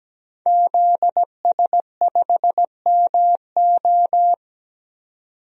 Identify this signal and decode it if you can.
Morse code, heard as a keyed tone, about 17 wpm, 700 Hz. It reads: ZS5MO
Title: DXCC-100 Call Signs - 5 Characters - 17wpm